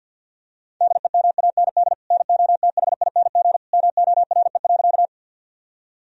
Morse code, heard as a keyed tone, about 50 wpm, 700 Hz. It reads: DEMAND NOTHING MORE.
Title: Sentences from Top 1000 Words - Review Part 4 - 50wpm